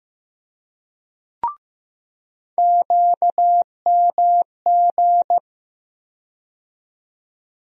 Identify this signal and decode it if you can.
Morse code, heard as a keyed tone, about 15 wpm, 700 Hz. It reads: QMG
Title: USA Call Sign Postfixes - 3 Letters 15wpm